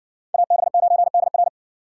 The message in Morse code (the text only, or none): AB9DR